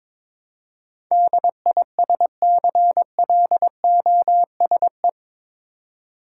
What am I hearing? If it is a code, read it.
Morse code, heard as a keyed tone, about 22 wpm, 700 Hz. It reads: DISCLOSE
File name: practice / 22wpm / Eight Letter Words - 22wpm